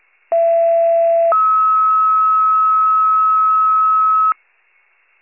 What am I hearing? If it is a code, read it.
Two-tone page, heard as a one-second tone, then a three-second tone, about 670 then 1290 Hz. It is heard from 0.3 to 4.3 s.